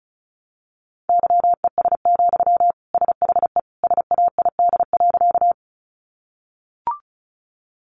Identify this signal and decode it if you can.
Morse code, heard as a keyed tone, about 34 wpm, 700 Hz. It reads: YES, SHE SAID.
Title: Sentences from Top 500 Words 35wpm